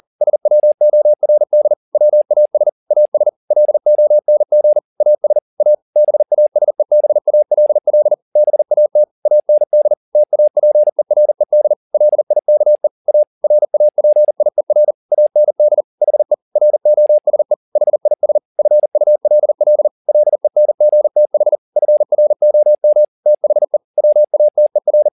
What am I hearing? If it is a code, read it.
Morse code, heard as a keyed tone, about 40 wpm, 600 Hz. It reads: SWORD WAS AS LONG AS A BASEBALL BAT AND TAPERED LIKE A RAPIER AND HE ROSE HIS FULL LENGTH FROM THE WATER